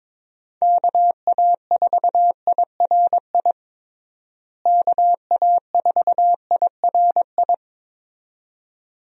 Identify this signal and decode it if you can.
Morse code, heard as a keyed tone, about 22 wpm, 700 Hz. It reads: KA4IRI KA4IRI
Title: USA Call Signs - 6 Characters - Repeated 2x - 22wpm